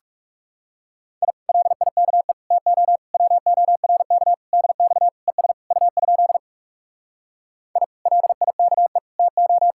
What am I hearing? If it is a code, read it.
Morse code, heard as a keyed tone, about 40 wpm, 700 Hz. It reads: I LIKE TO WORK DX ES U? I LIKE TO